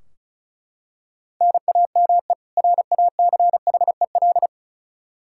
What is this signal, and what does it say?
Morse code, heard as a keyed tone, about 36 wpm, 700 Hz. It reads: NAME RACHEL
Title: Practice QSO - Exchange 1 - Beginning - 35wpm